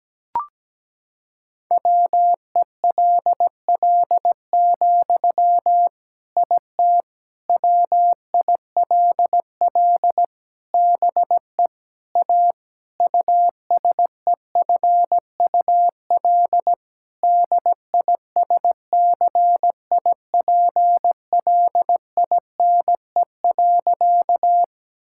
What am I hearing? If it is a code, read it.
Morse code, heard as a keyed tone, about 17 wpm, 700 Hz. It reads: WELL, IT WILL BE A USEFUL DISCIPLINE.
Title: Sentences from Top 1000 Words - Review Part 6 - 17wpm